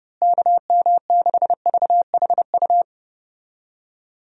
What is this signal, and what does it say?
Morse code, heard as a keyed tone, about 32 wpm, 700 Hz. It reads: KM6VHU
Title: USA Call Signs - 6 Characters - Repeated 3x - 30wpm